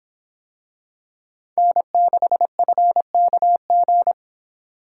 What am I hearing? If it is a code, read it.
Morse code, heard as a keyed tone, about 26 wpm, 700 Hz. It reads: N6FKG